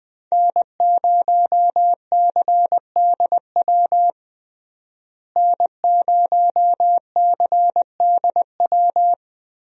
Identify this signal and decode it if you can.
Morse code, heard as a keyed tone, about 21 wpm, 700 Hz. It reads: N0CDW N0CDW
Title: USA Call Signs - 5 Characters - Repeated 2x - 20wpm